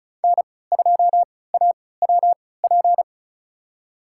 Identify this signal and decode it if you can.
Morse code, heard as a keyed tone, about 36 wpm, spaced as for 20 wpm, 700 Hz. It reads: N2AWP
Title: USA Call Signs - 5 Characters - 3x Character Spacing - 35wpm